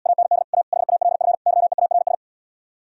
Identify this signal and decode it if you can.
Morse code, heard as a keyed tone, about 78 wpm, 700 Hz. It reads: AND A HALF YEARS